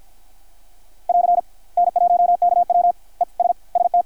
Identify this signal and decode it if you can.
Morse code, heard as a keyed tone, about 53 wpm, 720 Hz, with radio noise. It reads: <AR> N1KW ES ST
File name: archived / CRSnet